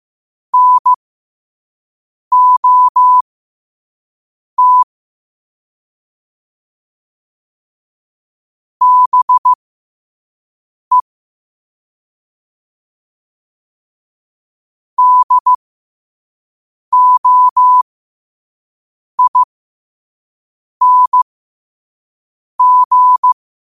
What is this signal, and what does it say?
Morse code, heard as a keyed tone, about 15 wpm, spaced as for 5 wpm, 1000 Hz. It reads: NOT BE DOING
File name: qotdfiles / 5wpm